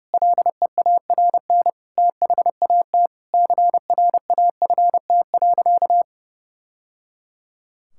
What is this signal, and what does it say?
Morse code, heard as a keyed tone, about 30 wpm, 700 Hz. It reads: LEARN THAT CRAFT.
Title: Sentences from Top 1000 Words - Review Part 6 - 30wpm